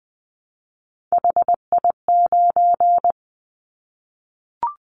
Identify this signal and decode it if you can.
Morse code, heard as a keyed tone, about 20 wpm, 700 Hz. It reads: HI9
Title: Callsign Prefixes for Top 100 DXCC - 20wpm